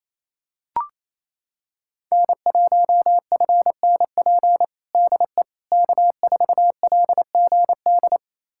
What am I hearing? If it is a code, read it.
Morse code, heard as a keyed tone, about 28 wpm, 700 Hz. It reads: N1FNP DE K4LGD
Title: USA Callsign DE Callsign - 5 Characters - 28wpm